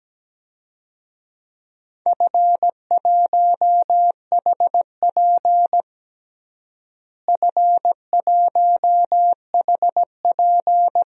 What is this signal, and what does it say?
Morse code, heard as a keyed tone, about 17 wpm, 700 Hz. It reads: F1HP F1HP
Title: DXCC-100 Call Signs - 4 Characters - Repeated 3x - 17wpm